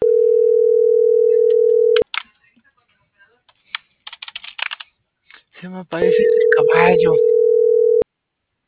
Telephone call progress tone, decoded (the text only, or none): ringback tone